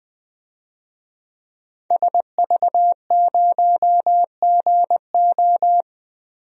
Morse code, heard as a keyed tone, about 20 wpm, 700 Hz. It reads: SV0GO
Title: DXCC-100 Call Signs - 5 Characters - 20wpm